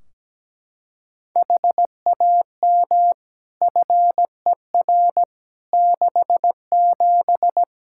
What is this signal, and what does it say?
Morse code, heard as a keyed tone, about 17 wpm, 700 Hz. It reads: HAM FER 67
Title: Practice QSO - Exchange 3 - More Info - 17wpm